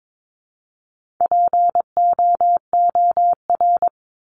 Morse code, heard as a keyed tone, about 22 wpm, 700 Hz. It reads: POOR
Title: Top 751-1000 Words 22wpm